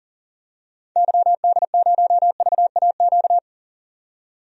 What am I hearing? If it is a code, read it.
Morse code, heard as a keyed tone, about 38 wpm, 700 Hz. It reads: YD0VAQ
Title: DXCC-100 Call Signs - 6 Characters - 40wpm